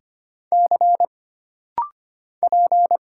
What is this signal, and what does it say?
Morse code, heard as a keyed tone, about 25 wpm, 700 Hz. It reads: C P
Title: Single Letters - Rapid-Fire 25wpm